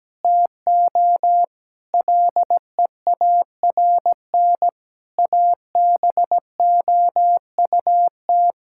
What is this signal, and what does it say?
Morse code, heard as a keyed tone, about 17 wpm, 700 Hz. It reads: TO LEARN ABOUT
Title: Sentences from Top 1000 Words - Review Part 6 - 17wpm